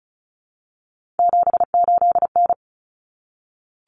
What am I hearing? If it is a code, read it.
Morse code, heard as a keyed tone, about 34 wpm, 700 Hz. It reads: 78N
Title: Common Cut Numbers - 3-Digit 35wpm